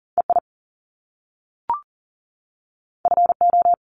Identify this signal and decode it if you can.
Morse code, heard as a keyed tone, about 40 wpm, 700 Hz. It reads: EI FO
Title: Top Two Letter Combinations in Words 40wpm